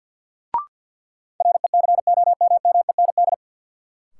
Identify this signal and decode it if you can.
Morse code, heard as a keyed tone, about 52 wpm, 700 Hz. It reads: RECOMMEND
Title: Sentences from Top 1200 Words - New Word Drill - 50wpm